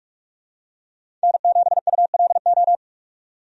Morse code, heard as a keyed tone, about 46 wpm, 700 Hz. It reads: N7ULO
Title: USA Call Signs - 5 Characters - 45wpm